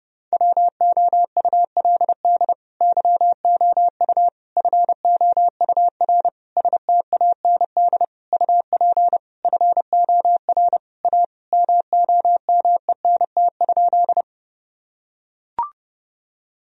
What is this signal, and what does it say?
Morse code, heard as a keyed tone, about 30 wpm, 700 Hz. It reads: WOULD YOU FOUR STAND UP FOR A MOMENT?
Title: Sentences from Top 1000 Words - Review Part 1 - 30wpm